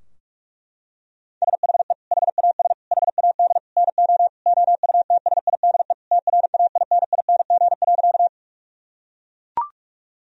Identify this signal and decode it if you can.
Morse code, heard as a keyed tone, about 41 wpm, 700 Hz. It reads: SHE HAS HAD NO OUTSIDE TRAINING.
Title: Sentences from Top 1000 Words - Review Part 4 - 45wpm